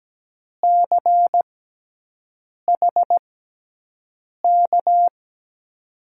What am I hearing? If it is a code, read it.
Morse code, heard as a keyed tone, about 17 wpm, spaced as for 6 wpm, 700 Hz. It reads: CHK
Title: USA Call Signs - 6 Characters - 6x Character Spacing - 17wpm